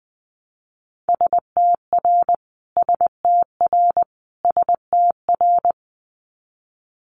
Morse code, heard as a keyed tone, about 20 wpm, 700 Hz. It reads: STR STR STR
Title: Multi-letter Phonemes Repeated 3x 20wpm